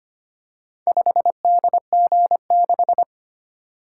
Morse code, heard as a keyed tone, about 25 wpm, 700 Hz. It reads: 5DG6
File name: practice / 25wpm / Four Letter-Number Combinations - 25wpm